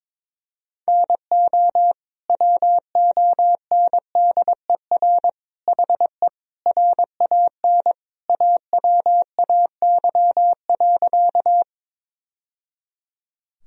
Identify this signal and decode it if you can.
Morse code, heard as a keyed tone, about 21 wpm, 700 Hz. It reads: NO WONDER HE RAN AWAY.
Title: Sentences from Top 1000 Words - Review Part 3 - 22wpm